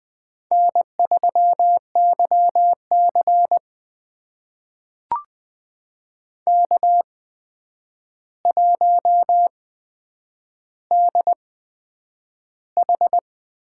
Morse code, heard as a keyed tone, about 20 wpm, spaced as for 9 wpm, 700 Hz. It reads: N3YC K1DH
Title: USA Call Signs - 4 Characters - 8x Character Spacing - 20wpm